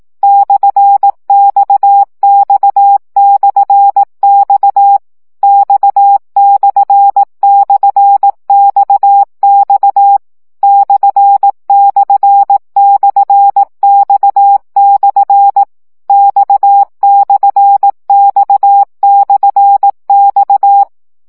Morse code, heard as a keyed tone, about 18 wpm, 800 Hz. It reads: /XX/X X//XX ///X/ X/X/X
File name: CW gyakorlas x per 15 baud